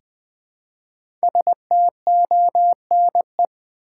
Morse code, heard as a keyed tone, about 20 wpm, 700 Hz. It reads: STONE
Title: 5 Letter Words - First 3 Letters Given - 20wpm